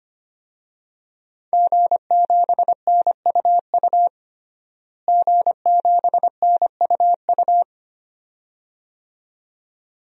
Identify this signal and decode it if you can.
Morse code, heard as a keyed tone, about 25 wpm, 700 Hz. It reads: G7NUU G7NUU
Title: DXCC-100 Call Signs - 5 Characters - Repeated 2x - 25wpm